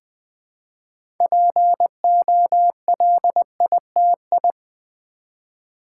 Morse code, heard as a keyed tone, about 20 wpm, 700 Hz. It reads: POLITI
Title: Build A Word - 2 Letters At A Time - 8-Letter Words - 20wpm